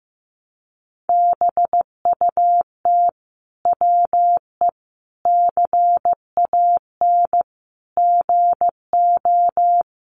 Morse code, heard as a keyed tone, about 15 wpm, 700 Hz. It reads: BUT WE CAN GO
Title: Sentences from Top 100 Words 15wpm